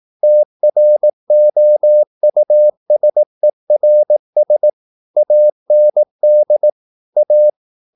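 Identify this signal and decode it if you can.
Morse code, heard as a keyed tone, about 18 wpm, 600 Hz. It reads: TROUSERS AND A
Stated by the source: Morse practice at 18 wpm, English text, from CW med Gnister.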